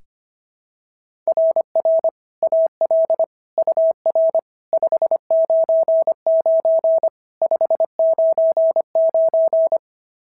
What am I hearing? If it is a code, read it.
Morse code, heard as a keyed tone, about 25 wpm, 690 Hz. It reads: RR AL UR 599 599
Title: SOTA QSO Practice 25wpm